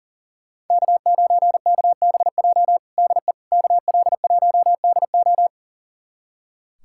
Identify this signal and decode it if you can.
Morse code, heard as a keyed tone, about 40 wpm, 700 Hz. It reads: K9KBJ DE KL1DO
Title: USA Callsign DE Callsign - 5 Characters - 40wpm